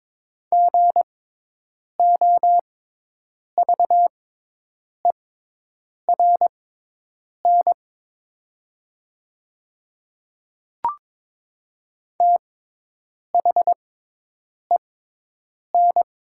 Morse code, heard as a keyed tone, about 22 wpm, spaced as for 8 wpm, 700 Hz. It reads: GOVERN THEN
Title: 6x Character Spacing - Top 500 Words - 22wpm